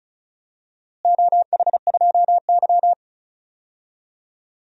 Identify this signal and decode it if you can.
Morse code, heard as a keyed tone, about 36 wpm, 700 Hz. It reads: OH2Y
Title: DXCC-100 Call Signs - 4 Characters - Repeated 2x - 35wpm